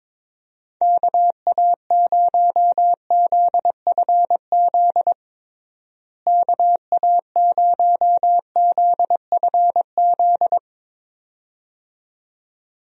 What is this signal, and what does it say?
Morse code, heard as a keyed tone, about 22 wpm, 700 Hz. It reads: KA0ZFZ KA0ZFZ
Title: USA Call Signs - 6 Characters - Repeated 2x - 22wpm